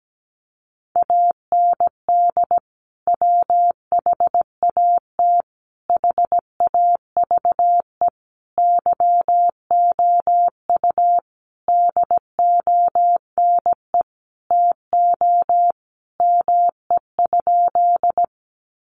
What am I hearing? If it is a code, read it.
Morse code, heard as a keyed tone, about 17 wpm, 700 Hz. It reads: AND WHAT HAVE YOU DONE TO ME?
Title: Sentences from Top 500 Words 17wpm